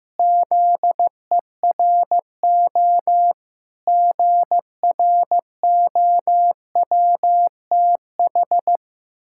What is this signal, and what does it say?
Morse code, heard as a keyed tone, about 15 wpm, 700 Hz. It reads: ZERO GROWTH
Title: Pangram Challenge v2 - 15wpm